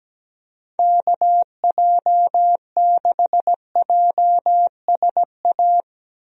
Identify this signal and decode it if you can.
Morse code, heard as a keyed tone, about 17 wpm, 700 Hz. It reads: KJ6JSA
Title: USA Call Signs - 6 Characters - Repeated 3x - 17wpm